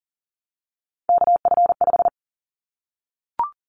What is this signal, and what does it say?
Morse code, heard as a keyed tone, about 40 wpm, 700 Hz. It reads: KF5
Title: Callsign Prefixes for Top 25 DXCC - 40wpm